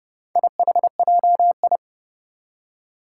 Morse code, heard as a keyed tone, about 30 wpm, 700 Hz. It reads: IHJI
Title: Four Challenging Letter-Number Combinations 30wpm